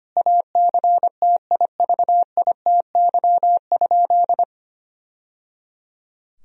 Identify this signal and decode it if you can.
Morse code, heard as a keyed tone, about 25 wpm, 700 Hz. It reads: ACTIVITY?
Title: Sentences from Top 1000 Words - Review Part 4 - 25wpm